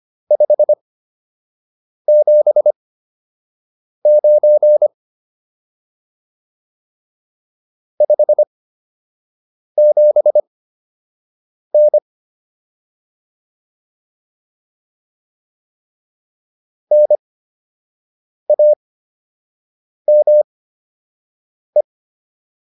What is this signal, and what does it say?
Morse code, heard as a keyed tone, about 25 wpm, spaced as for 6 wpm, 600 Hz. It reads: 579 57N NAME